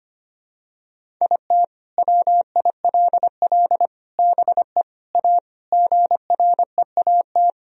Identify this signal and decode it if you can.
Morse code, heard as a keyed tone, about 25 wpm, 700 Hz. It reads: IT WILL BE A GREAT
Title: Sentences from Top 1000 Words - Review Part 1 - 25wpm